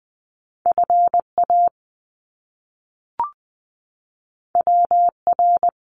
Morse code, heard as a keyed tone, about 20 wpm, 700 Hz. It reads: FA WR
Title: Top Two Letter Combinations in Words 20wpm